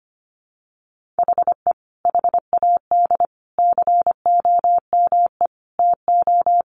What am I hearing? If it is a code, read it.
Morse code, heard as a keyed tone, about 25 wpm, 700 Hz. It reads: HE HAD COME TO
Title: Sentences from Top 1500 Words 25wpm